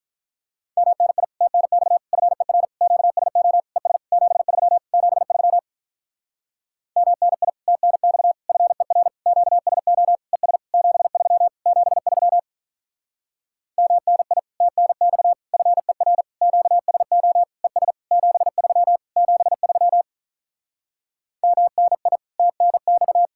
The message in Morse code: MNI TNX FER QSO ES 73 73 MNI TNX FER QSO ES 73 73 MNI TNX FER QSO ES 73 73 MNI TNX

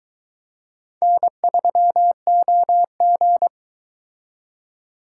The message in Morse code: N3OG

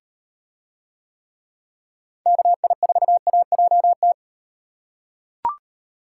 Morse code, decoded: KI4AJT

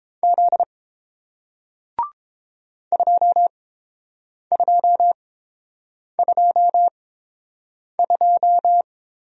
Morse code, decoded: Z 2222

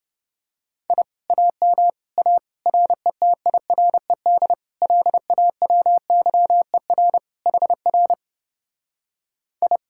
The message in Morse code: I AM A RETIRED LAWYER HR I

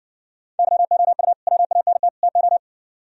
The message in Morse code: YOU WANT TO

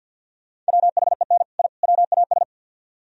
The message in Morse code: WHEN I WAS

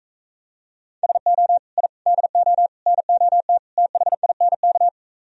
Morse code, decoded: SO I DO NOT THINK

